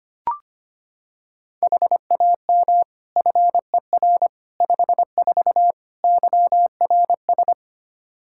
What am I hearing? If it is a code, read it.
Morse code, heard as a keyed tone, about 25 wpm, 700 Hz. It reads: HAM FER 54 YRS
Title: K1USN SST QSO Practice 25wpm